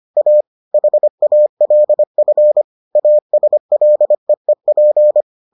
Morse code, heard as a keyed tone, about 25 wpm, 600 Hz. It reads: A HALF ASLEEP